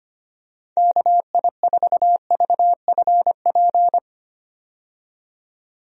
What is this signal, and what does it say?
Morse code, heard as a keyed tone, about 25 wpm, 700 Hz. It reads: KI4VFP
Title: USA Call Signs - 6 Characters - 25wpm